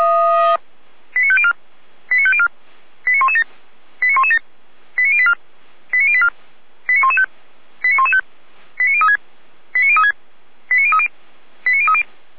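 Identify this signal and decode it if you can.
Five-tone selective call (ZVEI), heard as quick runs of five tones, tones of about 70 ms. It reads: 89504 89504 89117 89117 89084 89084 89115 89115 89036 89036 89030 89030
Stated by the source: Five-tone ZVEI selective call 89057, received on SDR.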